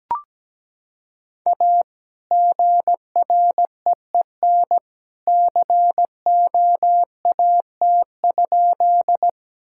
Morse code, heard as a keyed tone, about 17 wpm, 700 Hz. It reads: A GREEN COAT?
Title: Sentences from Top 1600 Words 17wpm